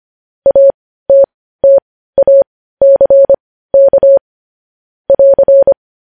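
Morse code, heard as a keyed tone, about 25 wpm, spaced as for 15 wpm, 550 Hz. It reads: ATTACK <AR>
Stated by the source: Morse code news headlines sent at 15 WPM.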